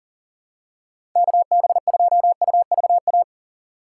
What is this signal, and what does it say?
Morse code, heard as a keyed tone, about 38 wpm, 700 Hz. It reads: KB2UVA
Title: USA Call Signs - 6 Characters - 4x Character Spacing - 40wpm